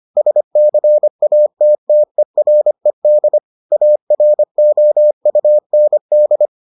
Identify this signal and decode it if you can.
Morse code, heard as a keyed tone, about 25 wpm, 600 Hz. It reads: SCATTERED AROUND